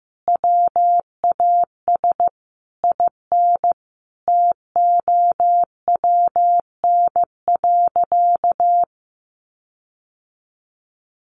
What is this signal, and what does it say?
Morse code, heard as a keyed tone, about 15 wpm, 700 Hz. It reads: WAS IN TOWN.
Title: Sentences from Top 1500 Words 15wpm